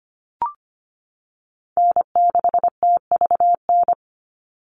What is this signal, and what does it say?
Morse code, heard as a keyed tone, about 25 wpm, 700 Hz. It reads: N6TVN
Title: Call Signs - 25wpm